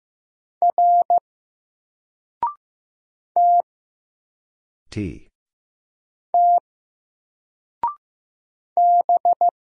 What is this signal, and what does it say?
Morse code, heard as a keyed tone, about 15 wpm, 700 Hz. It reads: R T T B
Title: Single Letter-Number 15wpm